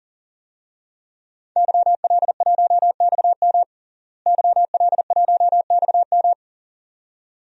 Morse code, heard as a keyed tone, about 40 wpm, 700 Hz. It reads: YL1XM YL1XM
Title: DXCC-100 Call Signs - 5 Characters - Repeated 2x - 40wpm